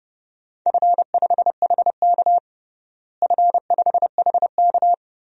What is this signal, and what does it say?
Morse code, heard as a keyed tone, about 32 wpm, 700 Hz. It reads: F5HK F5HK
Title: DXCC-100 Call Signs - 4 Characters - Repeated 2x - 30wpm